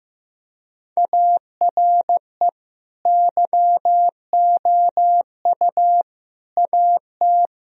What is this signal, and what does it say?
Morse code, heard as a keyed tone, about 15 wpm, 700 Hz. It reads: ARE YOU AT
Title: Sentences from Top 1000 Words - Review Part 4 - 15wpm